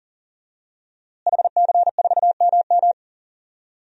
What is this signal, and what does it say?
Morse code, heard as a keyed tone, about 39 wpm, 700 Hz. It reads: HC4MM
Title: DXCC-100 Call Signs - 5 Characters - Repeated 2x - 40wpm